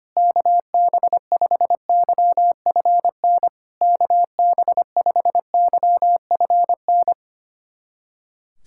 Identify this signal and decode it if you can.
Morse code, heard as a keyed tone, about 25 wpm, 700 Hz. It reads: KB5YFN KB5YFN
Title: SOTA QSO Practice 25wpm